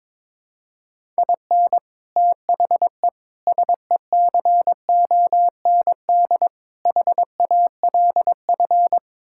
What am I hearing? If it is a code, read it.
Morse code, heard as a keyed tone, about 22 wpm, 700 Hz. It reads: IN THE SECOND HALF
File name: practice / 22wpm / Sets of 6 Words - 22wpm